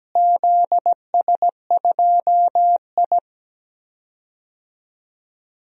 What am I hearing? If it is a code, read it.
Morse code, heard as a keyed tone, about 17 wpm, 700 Hz. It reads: ZS2I
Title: DXCC-100 Call Signs - 4 Characters - Repeated 3x - 17wpm